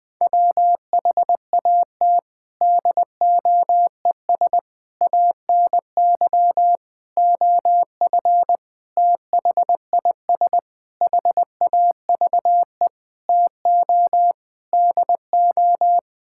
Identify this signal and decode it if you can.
Morse code, heard as a keyed tone, about 21 wpm, 700 Hz. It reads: WHAT DOES ANY OF THIS HAVE TO DO
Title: Sets of 9 Words - 20wpm